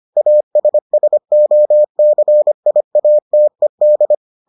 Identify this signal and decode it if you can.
Morse code, heard as a keyed tone, about 25 wpm, 600 Hz. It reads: ASSOCIATED